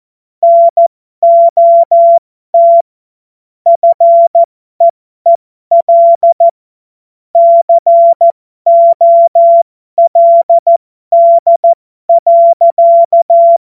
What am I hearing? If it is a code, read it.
Morse code, heard as a keyed tone, about 14 wpm, spaced as for 12 wpm, 680 Hz. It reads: NOT FEEL COLD.
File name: Morsekurs / Witze